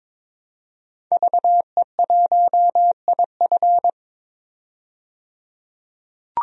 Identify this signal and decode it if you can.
Morse code, heard as a keyed tone, about 22 wpm, 700 Hz. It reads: VE1IF
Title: DXCC-100 Call Signs - 5 Characters - 22wpm